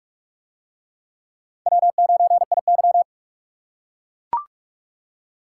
Morse code, heard as a keyed tone, about 42 wpm, 700 Hz. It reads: W9IY